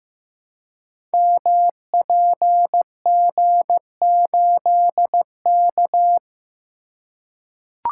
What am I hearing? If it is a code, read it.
Morse code, heard as a keyed tone, about 15 wpm, 700 Hz. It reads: MPG8K
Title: Code Groups - 5 Letter:Numbers - 15wpm